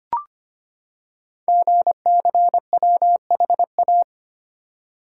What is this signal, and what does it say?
Morse code, heard as a keyed tone, about 25 wpm, 700 Hz. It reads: GCWHA
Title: Code Groups - 5 Letter:Numbers - 25wpm